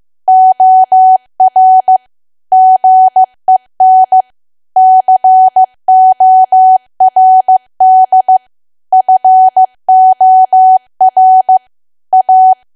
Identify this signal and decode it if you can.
Morse code, heard as a keyed tone, about 15 wpm, 750 Hz. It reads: OR GEN CORD FOR A
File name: cwa / W1AW-Level2 / 15WPM